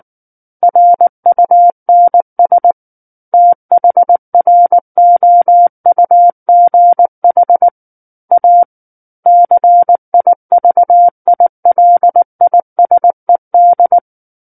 Morse code, heard as a keyed tone, about 19 wpm, 700 Hz. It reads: RUNS THROUGH A CIVILISED
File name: Book One / 10-Chapter 10 - 19 WPM